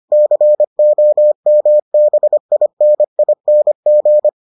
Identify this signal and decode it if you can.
Morse code, heard as a keyed tone, about 25 wpm, 600 Hz. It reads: COMBINING